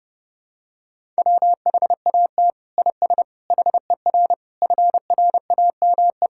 Morse code, heard as a keyed tone, about 32 wpm, 700 Hz. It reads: WHAT IS HER FRAME
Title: Sentences from Top 1000 Words - Review Part 6 - 30wpm